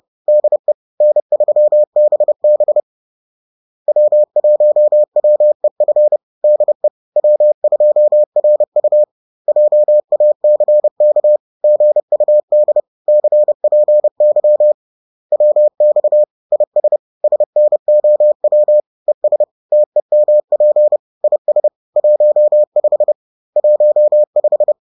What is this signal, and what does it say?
Morse code, heard as a keyed tone, about 30 wpm, 600 Hz. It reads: DE N3BB W1WEF DE W2RU JACK GUD CPY WX IS SNOW ES TEMP IS 15 15
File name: 30wpm